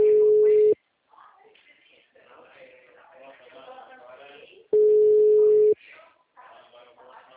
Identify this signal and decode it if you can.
Ringback tone (Europe, 425 Hz), with speech.